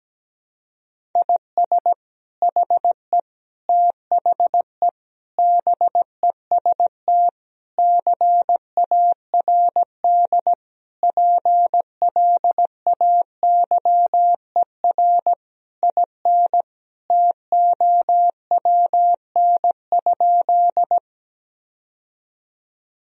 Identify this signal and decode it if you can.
Morse code, heard as a keyed tone, about 17 wpm, 700 Hz. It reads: IS HE THE BEST CARD PLAYER IN TOWN?
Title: Sentences from Top 1000 Words - Review Part 4 - 17wpm